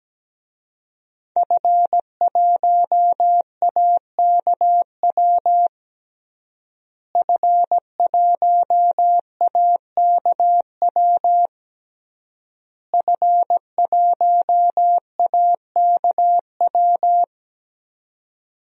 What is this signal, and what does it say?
Morse code, heard as a keyed tone, about 17 wpm, 700 Hz. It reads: F1AKW F1AKW F1AKW
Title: DXCC-100 Call Signs - 5 Characters - Repeated 3x - 17wpm